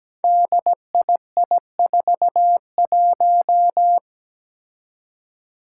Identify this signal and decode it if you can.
Morse code, heard as a keyed tone, about 17 wpm, 700 Hz. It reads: DII41
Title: Five Challenging Letter-Number Combinations 17wpm